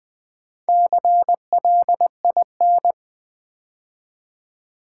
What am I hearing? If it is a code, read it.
Morse code, heard as a keyed tone, about 20 wpm, 700 Hz. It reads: CLIN
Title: Build A Word - 2 Letters At A Time - 6-Letter Words - 20wpm